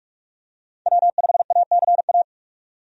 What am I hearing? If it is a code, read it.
Morse code, heard as a keyed tone, about 46 wpm, 700 Hz. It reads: W5ACA
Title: USA Call Signs - 5 Characters - 45wpm